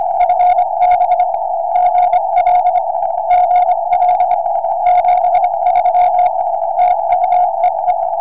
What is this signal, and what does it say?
Morse code, heard as a keyed tone, about 25 wpm, 720 Hz. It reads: FB GL GL 73 TU EE
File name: tropo / es5pc